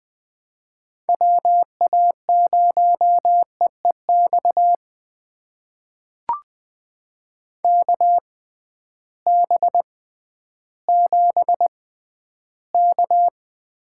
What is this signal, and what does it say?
Morse code, heard as a keyed tone, about 20 wpm, spaced as for 12 wpm, 700 Hz. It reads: WA0EEX KB7K
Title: USA Call Signs - 6 Characters - 6x Character Spacing - 20wpm